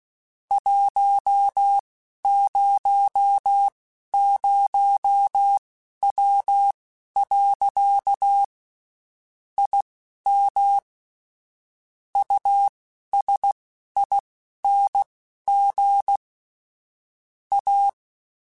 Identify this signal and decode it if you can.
Morse code, heard as a keyed tone, about 16 wpm, spaced as for 11 wpm, 800 Hz. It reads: 100W. IM USING A
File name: (spaced) / QSOs Spaced-mp3s / WD0FBT 1 @15 sp